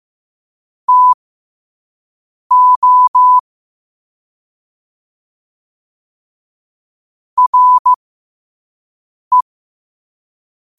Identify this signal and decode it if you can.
Morse code, heard as a keyed tone, about 15 wpm, spaced as for 5 wpm, 1000 Hz. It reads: TO RE
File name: qotdfiles / 5wpm